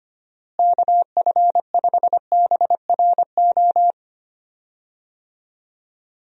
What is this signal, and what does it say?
Morse code, heard as a keyed tone, about 25 wpm, 700 Hz. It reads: KF5BRO
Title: USA Call Signs - 6 Characters - Repeated 3x - 25wpm